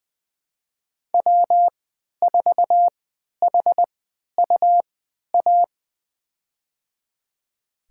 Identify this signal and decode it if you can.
Morse code, heard as a keyed tone, about 20 wpm, spaced as for 11 wpm, 700 Hz. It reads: W4HUA
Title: USA Call Signs - 5 Characters - 3x Character Spacing - 20wpm